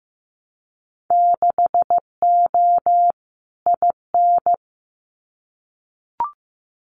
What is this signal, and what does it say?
Morse code, heard as a keyed tone, about 15 wpm, 700 Hz. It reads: 6O IN
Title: Winter Field Day Exchanges Encore 15wpm